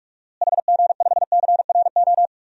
Morse code, heard as a keyed tone, about 44 wpm, 700 Hz. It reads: HG5CRO
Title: DXCC-100 Call Signs - 6 Characters - Repeated 3x - 45wpm